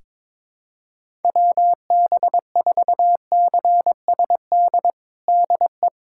Morse code, heard as a keyed tone, about 22 wpm, 700 Hz. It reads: WB4CSD DE
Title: USA Callsign DE Callsign 22wpm